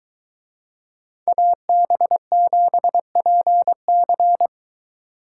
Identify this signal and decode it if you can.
Morse code, heard as a keyed tone, about 22 wpm, 700 Hz. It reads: AB7PC